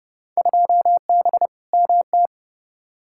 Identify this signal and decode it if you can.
Morse code, heard as a keyed tone, about 31 wpm, 700 Hz. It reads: 2B MT
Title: Field Day Exchanges Encore - 30wpm